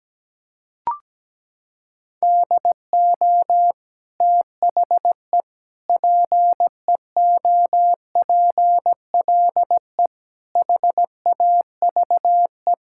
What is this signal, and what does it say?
Morse code, heard as a keyed tone, about 17 wpm, 700 Hz. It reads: DO THE PEOPLE HAVE
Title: Sentences from Top 1000 Words - Review Part 5 - 17wpm